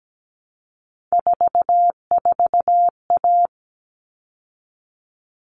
Morse code, heard as a keyed tone, about 17 wpm, 700 Hz. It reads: 44A
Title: All Cut Numbers - 3-Digit 17wpm